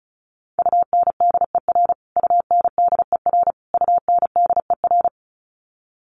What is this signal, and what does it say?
Morse code, heard as a keyed tone, about 34 wpm, 700 Hz. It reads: UNDER UNDER UNDER
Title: Top 200 Words Repeated 35wpm